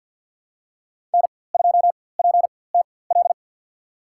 Morse code, heard as a keyed tone, about 53 wpm, spaced as for 24 wpm, 700 Hz. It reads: N2PTL
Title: USA Call Signs - 4x Character Spacing - 50wpm